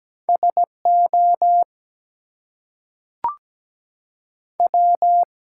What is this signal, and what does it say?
Morse code, heard as a keyed tone, about 17 wpm, 700 Hz. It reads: SO W
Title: Two Letter Words - 17wpm